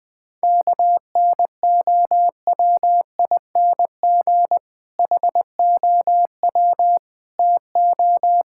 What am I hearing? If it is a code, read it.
Morse code, heard as a keyed tone, about 20 wpm, 700 Hz. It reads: KNOWING HOW TO
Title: Difficult Sentences with Hints - 20wpm